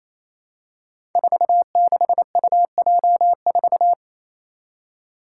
Morse code, heard as a keyed tone, about 29 wpm, 700 Hz. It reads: 46UJ4
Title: Five Challenging Letter-Number Combinations 28wpm